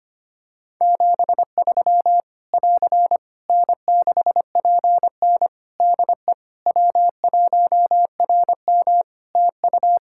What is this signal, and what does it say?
Morse code, heard as a keyed tone, about 25 wpm, 700 Hz. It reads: 73 <AR> N6PN DE W1RM TU
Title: Practice QSO - Exchange 4 - Ending - 25wpm